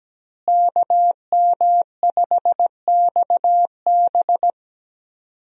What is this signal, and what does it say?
Morse code, heard as a keyed tone, about 17 wpm, 700 Hz. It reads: KM5XB
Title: USA Call Signs - 5 Characters - Repeated 3x - 17wpm